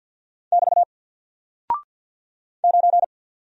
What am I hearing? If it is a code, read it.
Morse code, heard as a keyed tone, about 48 wpm, 700 Hz. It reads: <BT> 9
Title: Single Character (Letter:Number:Punctuation:Prosign) Alt 50wpm